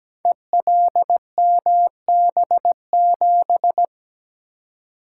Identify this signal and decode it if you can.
Morse code, heard as a keyed tone, about 17 wpm, 700 Hz. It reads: ELMB7
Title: Code Groups - 5 Letter:Numbers - 17wpm